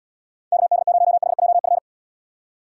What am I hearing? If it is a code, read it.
Morse code, heard as a keyed tone, about 74 wpm, 700 Hz. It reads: KD0HJF